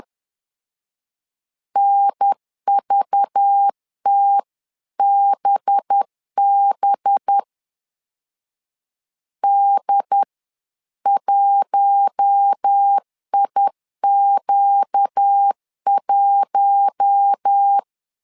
Morse code, heard as a keyed tone, about 10 wpm, 800 Hz. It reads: NVTBB D1IQ1
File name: LEZIONI / 5 LBVC / 5 LBV 10wpm 2248